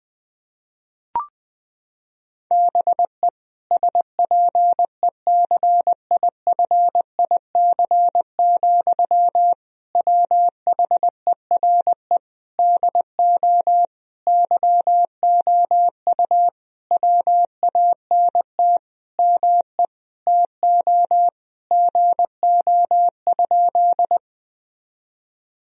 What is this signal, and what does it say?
Morse code, heard as a keyed tone, about 20 wpm, 700 Hz. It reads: BE SPECIFIC, WHERE DO YOU WANT ME TO GO?
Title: Sentences from Top 1000 Words - Review Part 3 - 20wpm